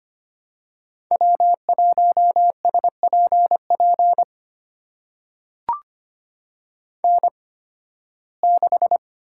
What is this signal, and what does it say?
Morse code, heard as a keyed tone, about 25 wpm, spaced as for 16 wpm, 700 Hz. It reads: W1SPP N6
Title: USA Call Signs - 5 Characters - 8x Character Spacing - 25wpm